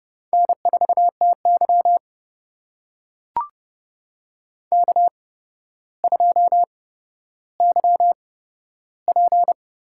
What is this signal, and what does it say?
Morse code, heard as a keyed tone, about 30 wpm, spaced as for 13 wpm, 700 Hz. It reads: N4TY K2YP
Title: USA Call Signs - 4 Characters - 8x Character Spacing - 30wpm